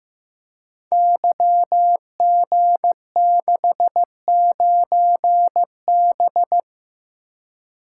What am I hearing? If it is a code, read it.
Morse code, heard as a keyed tone, about 15 wpm, 700 Hz. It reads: YG69B
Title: Code Groups - 5 Letter:Numbers - 15wpm